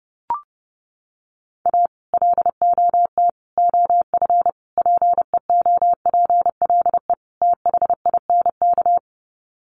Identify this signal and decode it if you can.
Morse code, heard as a keyed tone, about 30 wpm, 700 Hz. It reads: A LOT OF PEOPLE THINK
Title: Sets of 5 Words Encore 30wpm